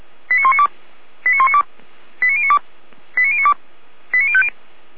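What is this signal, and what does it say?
Five-tone selective call (ZVEI), heard as quick runs of five tones, tones of about 80 ms. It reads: 89192 89192 89092 89092 89050
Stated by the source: ZVEI five-tone selective call received by SDR.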